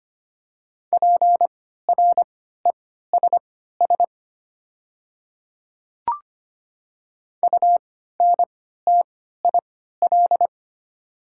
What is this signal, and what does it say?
Morse code, heard as a keyed tone, about 25 wpm, spaced as for 14 wpm, 700 Hz. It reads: PRESS UNTIL
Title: 3x Character Spacing - Top 300 Words - 25wpm